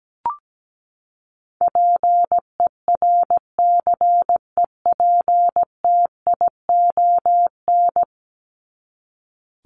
Morse code, heard as a keyed tone, about 17 wpm, 700 Hz. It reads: PERCEPTION
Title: Top 1751-2000 Words 17wpm